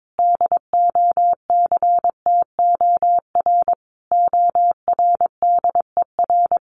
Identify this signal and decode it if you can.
Morse code, heard as a keyed tone, about 22 wpm, 700 Hz. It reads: DOCTOR ORDER
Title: Sentences from Top 1500 Words 22wpm